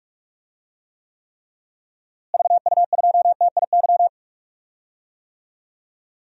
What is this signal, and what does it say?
Morse code, heard as a keyed tone, about 45 wpm, 700 Hz. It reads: VU2TIY
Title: DXCC-100 Call Signs - 6 Characters - Repeated 3x - 45wpm